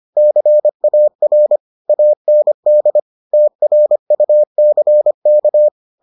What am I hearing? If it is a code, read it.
Morse code, heard as a keyed tone, about 25 wpm, 600 Hz. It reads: CAR AND TRUCK